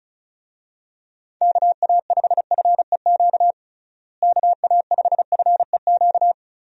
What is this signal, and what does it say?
Morse code, heard as a keyed tone, about 34 wpm, 700 Hz. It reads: KA5FEQ KA5FEQ
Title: USA Call Signs - 6 Characters - Repeated 3x - 35wpm